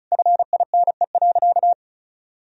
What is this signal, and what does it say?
Morse code, heard as a keyed tone, about 36 wpm, 700 Hz. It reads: FINE.
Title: Sentences from Top 1000 Words - Review Part 2 - 35wpm